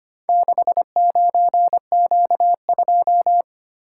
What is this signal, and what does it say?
Morse code, heard as a keyed tone, about 25 wpm, 700 Hz. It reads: 69Q2
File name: practice / 25wpm / Four Letter-Number Combinations - 25wpm